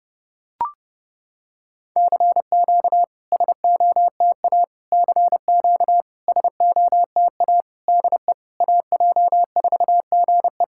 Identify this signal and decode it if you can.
Morse code, heard as a keyed tone, about 30 wpm, 690 Hz. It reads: CQ SOTA CQ SOTA DE AJ4GE
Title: SOTA QSO Practice 30wpm